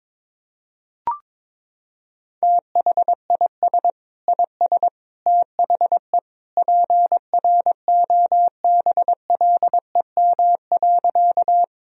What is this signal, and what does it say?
Morse code, heard as a keyed tone, about 22 wpm, 700 Hz. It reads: THIS IS THE PROBLEM.
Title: Sentences from Top 1000 Words - Review Part 2 - 22wpm